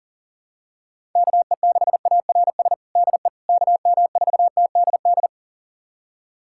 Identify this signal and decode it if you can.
Morse code, heard as a keyed tone, about 40 wpm, 700 Hz. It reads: KE6ARS DE KM4TDD
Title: USA Callsign DE Callsign - 6 Characters - 40wpm